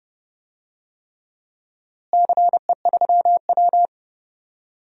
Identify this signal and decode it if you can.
Morse code, heard as a keyed tone, about 30 wpm, 700 Hz. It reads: CE3W
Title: DXCC-100 Call Signs - 4 Characters - Repeated 3x - 30wpm